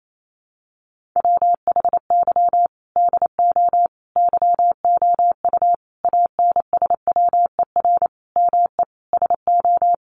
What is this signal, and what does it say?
Morse code, heard as a keyed tone, about 28 wpm, 700 Hz. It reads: WHY DO YOU ANSWER ME SO
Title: Sentences from Top 1500 Words 28wpm